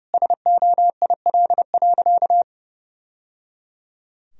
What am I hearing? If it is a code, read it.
Morse code, heard as a keyed tone, about 30 wpm, 700 Hz. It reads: SOIL.
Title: Sentences from Top 1000 Words - Review Part 4 - 30wpm